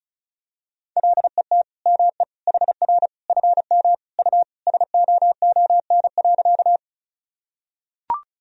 Morse code, heard as a keyed tone, about 36 wpm, 700 Hz. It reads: LET ME HR FM U SOON.
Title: Sentences from Top 400 Words and Abbreviations 35wpm